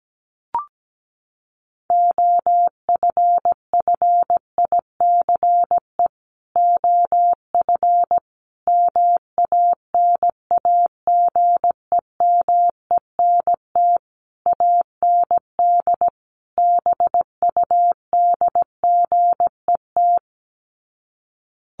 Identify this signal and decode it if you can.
Morse code, heard as a keyed tone, about 17 wpm, 700 Hz. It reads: OFFICE OF MANAGEMENT AND BUDGET
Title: Sets of 5 Words Encore2 17wpm